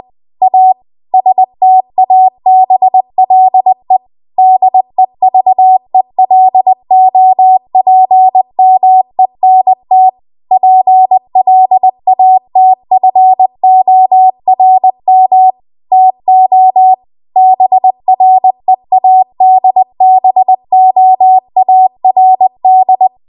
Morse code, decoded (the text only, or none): A STABLE DEVELOPMENT PLATFORM TO BREADBOARD